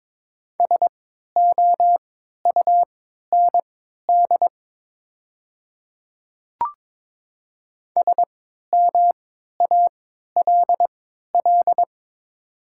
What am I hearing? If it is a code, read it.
Morse code, heard as a keyed tone, about 22 wpm, spaced as for 13 wpm, 700 Hz. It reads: SOUND SMALL
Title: 3x Character Spacing - Top 200 Words - 22wpm